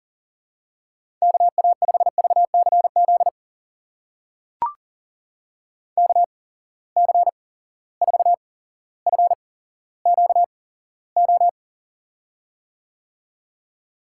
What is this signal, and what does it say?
Morse code, heard as a keyed tone, about 42 wpm, spaced as for 17 wpm, 700 Hz. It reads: KA5VCZ KC4FQO
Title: USA Call Signs - 6 Characters - 8x Character Spacing - 40wpm